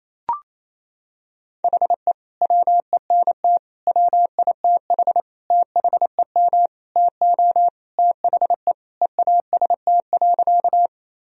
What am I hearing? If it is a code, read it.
Morse code, heard as a keyed tone, about 28 wpm, 700 Hz. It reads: HE WENT WITH THEM TO THE EAST.
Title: Sentences from Top 1000 Words - Review Part 3 - 28wpm